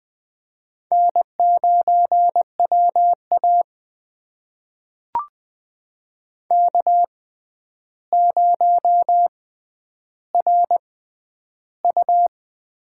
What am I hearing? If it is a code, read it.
Morse code, heard as a keyed tone, about 20 wpm, spaced as for 10 wpm, 700 Hz. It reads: N9WA K0RU
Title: USA Call Signs - 4 Characters - 6x Character Spacing - 20wpm